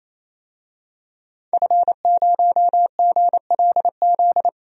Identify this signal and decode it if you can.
Morse code, heard as a keyed tone, about 28 wpm, 700 Hz. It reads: F0GLZ
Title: DXCC-100 Call Signs - 5 Characters - 28wpm